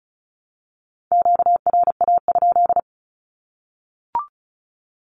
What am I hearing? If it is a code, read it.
Morse code, heard as a keyed tone, about 34 wpm, 700 Hz. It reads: QRA?
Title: Q-Codes Repeated 3x 35wpm